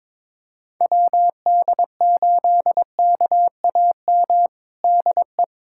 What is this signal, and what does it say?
Morse code, heard as a keyed tone, about 22 wpm, 700 Hz. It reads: WD8KAM DE
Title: USA Callsign DE Callsign 22wpm